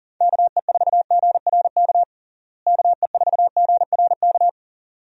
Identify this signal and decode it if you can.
Morse code, heard as a keyed tone, about 40 wpm, 700 Hz. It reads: KE4GRK KE4GRK
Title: USA Call Signs - 6 Characters - Repeated 2x - 40wpm